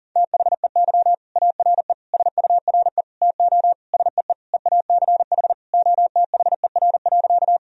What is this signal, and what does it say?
Morse code, heard as a keyed tone, about 40 wpm, 700 Hz. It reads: THEY ARE SURE TO SEE EACH OTHER.
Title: Sentences from Top 1000 Words - Review Part 1 - 40wpm